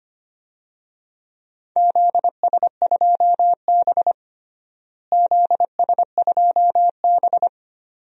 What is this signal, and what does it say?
Morse code, heard as a keyed tone, about 25 wpm, 700 Hz. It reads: ZS2B ZS2B
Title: DXCC-100 Call Signs - 4 Characters - Repeated 2x - 25wpm